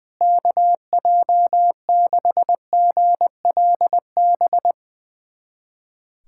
Morse code, heard as a keyed tone, about 21 wpm, 700 Hz. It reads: KJ6GLB
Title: USA Callsign DE Callsign - 6 Characters - 20wpm